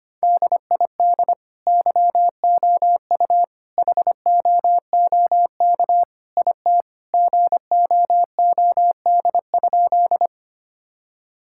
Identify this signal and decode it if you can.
Morse code, heard as a keyed tone, about 25 wpm, 700 Hz. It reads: DID YOU HOOK IT GOOD?
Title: Sentences from Top 1600 Words 25wpm